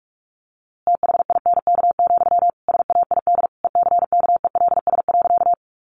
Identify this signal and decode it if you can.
Morse code, heard as a keyed tone, about 44 wpm, 700 Hz. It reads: THINK, SAID ECKELS.
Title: A Sound Of Thunder 45wpm